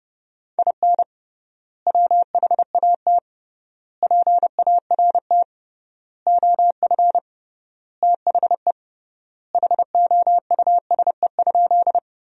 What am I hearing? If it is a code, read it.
Morse code, heard as a keyed tone, about 30 wpm, 700 Hz. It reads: IN WHAT PART OF THE HOUSE?
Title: 3x Word-Spacing - Sentences from Top 200 Words - 30wpm